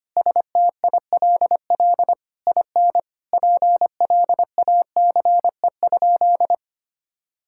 Morse code, heard as a keyed tone, about 25 wpm, 700 Hz. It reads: STILL IN PLACE?
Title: Sentences from Top 1000 Words - Review Part 6 - 25wpm